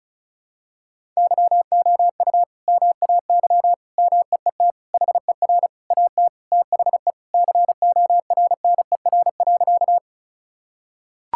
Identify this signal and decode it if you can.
Morse code, heard as a keyed tone, about 34 wpm, 700 Hz. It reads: YOU MAY MEET HER AT THE CORNER.
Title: Sentences from Top 1600 Words 35wpm